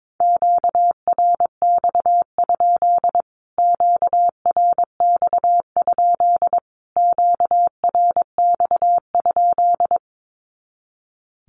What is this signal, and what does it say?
Morse code, heard as a keyed tone, about 22 wpm, 700 Hz. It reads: QRX? QRX? QRX?
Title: Q-Codes Repeated 3x 22wpm